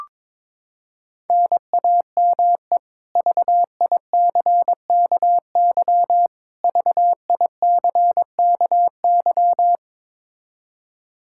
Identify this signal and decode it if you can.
Morse code, heard as a keyed tone, about 22 wpm, 700 Hz. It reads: NAME VICKY VICKY
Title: K1USN SST QSO Practice 22wpm